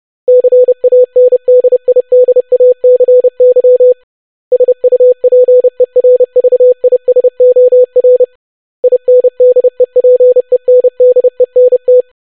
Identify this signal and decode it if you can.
Morse code, heard as a keyed tone, about 30 wpm, 500 Hz. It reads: CANDIDACY SUPERVISOR INDEPENDENT
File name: longwords-25wpm